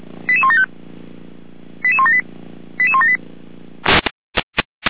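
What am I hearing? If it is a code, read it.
Five-tone selective call (ZVEI), heard as quick runs of five tones, tones of about 70 ms. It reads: 88175 88178 88178
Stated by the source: Five-tone ZVEI selective call received by SDR.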